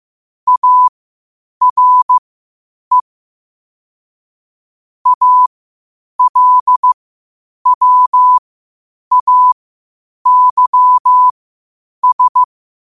Morse code, heard as a keyed tone, about 15 wpm, spaced as for 8 wpm, 1000 Hz. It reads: ARE ALWAYS
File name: qotdfiles / 8wpm